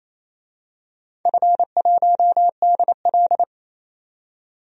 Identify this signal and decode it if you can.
Morse code, heard as a keyed tone, about 28 wpm, 700 Hz. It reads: F1DL
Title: DXCC-100 Call Signs - 4 Characters - 28wpm